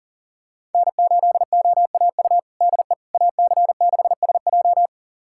Morse code, heard as a keyed tone, about 40 wpm, 700 Hz. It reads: N8OAU DE AC6SJ
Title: USA Callsign DE Callsign - 5 Characters - 40wpm